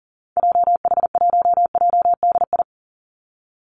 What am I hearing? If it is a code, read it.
Morse code, heard as a keyed tone, about 40 wpm, 700 Hz. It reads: JH1JDI
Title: Call Signs - 40wpm